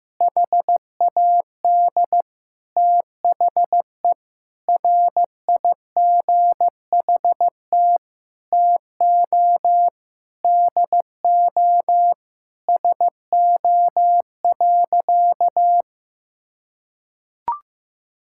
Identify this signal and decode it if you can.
Morse code, heard as a keyed tone, about 15 wpm, 700 Hz. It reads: HAD THE RIGHT TO DO SO.
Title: Sentences from Top 1000 Words - Review Part 6 - 15wpm